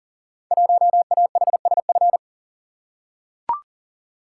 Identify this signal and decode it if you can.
Morse code, heard as a keyed tone, about 42 wpm, 700 Hz. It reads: 1AHSF
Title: Code Groups - 5 Letter:Numbers - 40wpm